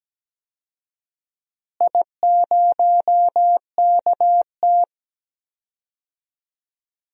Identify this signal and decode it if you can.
Morse code, heard as a keyed tone, about 17 wpm, 700 Hz. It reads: I0KT
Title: DXCC-100 Call Signs - 4 Characters - Repeated 3x - 17wpm